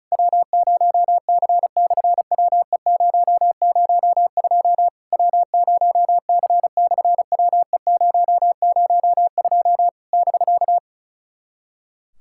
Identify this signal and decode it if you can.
Morse code, heard as a keyed tone, about 36 wpm, 700 Hz. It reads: W0C/WE002 W0C/WE002 <BK>
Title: SOTA QSO Practice 35wpm